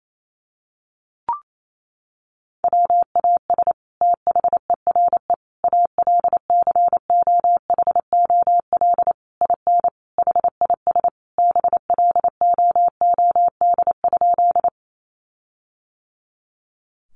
Morse code, heard as a keyed tone, about 28 wpm, 700 Hz. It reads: WAS THERE ALCOHOL IN HIS BLOOD?
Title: Sentences from Top 1400 Words 28wpm